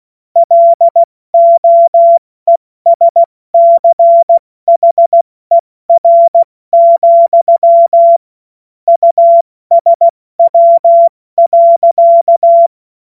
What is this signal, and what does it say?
Morse code, heard as a keyed tone, about 16 wpm, 680 Hz. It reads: LOESCHER, USW.
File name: Morsekurs / Witze / Witz 14. Juni 2022 70 F80